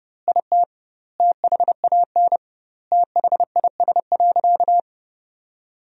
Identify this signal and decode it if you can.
Morse code, heard as a keyed tone, about 30 wpm, 700 Hz. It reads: IT THAN THIS.
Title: 2x Word-Spacing - Sentences from Top 100 Words - 30wpm